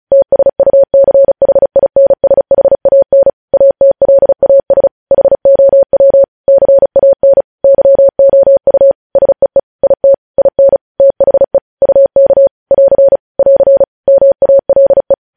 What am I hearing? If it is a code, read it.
Morse code, heard as a keyed tone, about 36 wpm, 550 Hz. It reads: TSUCHINSHAN ATLAS HOW CAN YOU SEE IT IN THE UK <AR> <AR> MALE